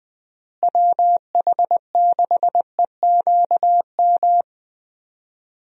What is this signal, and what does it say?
Morse code, heard as a keyed tone, about 20 wpm, 700 Hz. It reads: WH6EQM
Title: USA Call Signs - 6 Characters - Repeated 2x - 20wpm